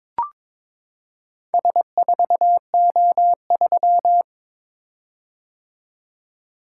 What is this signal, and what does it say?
Morse code, heard as a keyed tone, about 22 wpm, 700 Hz. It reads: S4O3
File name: practice / 22wpm / Four Letter-Number Combinations - 22wpm